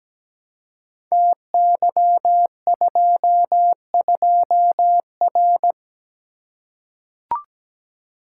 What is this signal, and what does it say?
Morse code, heard as a keyed tone, about 17 wpm, 700 Hz. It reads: TY22R
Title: Code Groups - 5 Letter:Numbers - 17wpm